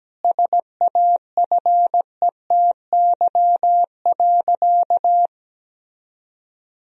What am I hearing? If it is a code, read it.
Morse code, heard as a keyed tone, about 17 wpm, 700 Hz. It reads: SAFETY.
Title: Sentences from Top 1000 Words - Review Part 4 - 17wpm